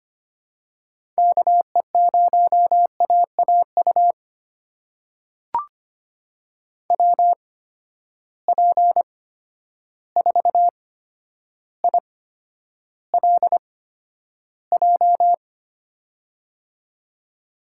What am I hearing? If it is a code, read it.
Morse code, heard as a keyed tone, about 25 wpm, spaced as for 11 wpm, 700 Hz. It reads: KE0AAU WP4ILJ
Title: USA Call Signs - 6 Characters - 8x Character Spacing - 25wpm